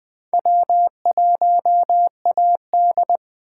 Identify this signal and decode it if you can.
Morse code, heard as a keyed tone, about 23 wpm, 700 Hz. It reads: W1AD
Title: Speed Racing - USA 1x2 Call Signs 20wpm